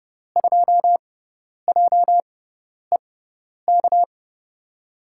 Morse code, heard as a keyed tone, about 30 wpm, spaced as for 10 wpm, 700 Hz. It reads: 2JEK
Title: USA Call Signs - 6x Character Spacing - 30wpm